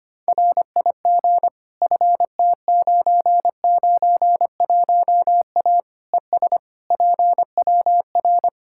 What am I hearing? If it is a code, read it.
Morse code, heard as a keyed tone, about 25 wpm, 700 Hz. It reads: RIG FT991A ES PWR
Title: Practice QSO - Exchange 2 - Info - 25wpm